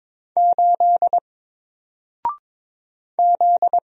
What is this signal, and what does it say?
Morse code, heard as a keyed tone, about 22 wpm, 700 Hz. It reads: 8 Z
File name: Single-Letter-Number-22wpm